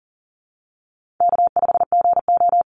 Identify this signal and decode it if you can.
Morse code, heard as a keyed tone, about 40 wpm, 700 Hz. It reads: K5GO
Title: Easy Call Signs Repeated 2x - 40wpm